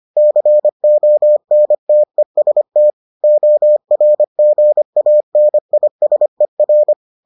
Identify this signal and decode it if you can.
Morse code, heard as a keyed tone, about 25 wpm, 600 Hz. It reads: CONTEST ORGANISER